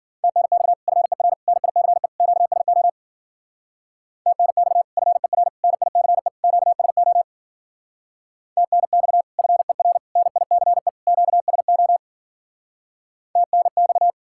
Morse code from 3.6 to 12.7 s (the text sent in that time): TNX FER NICE QSO TNX FER NICE QSO